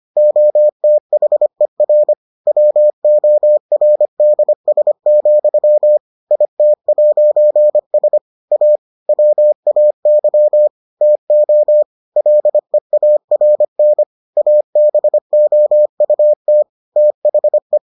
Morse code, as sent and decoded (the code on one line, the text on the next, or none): --- - .... .-. / .-- --- .-. -.. ... --..-- / .. - .----. ... / .- / .-- .- -.-- / - --- / .-.. .- .-. -. / .- -... --- ..- - / - .... .
OTHER WORDS, IT'S A WAY TO LEARN ABOUT THE